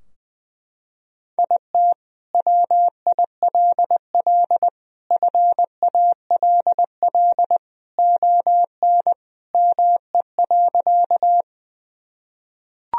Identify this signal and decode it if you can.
Morse code, heard as a keyed tone, about 19 wpm, 700 Hz. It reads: IT WILL FALL ON ME.
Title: Sentences from Top 1000 Words - Review Part 2 - 20wpm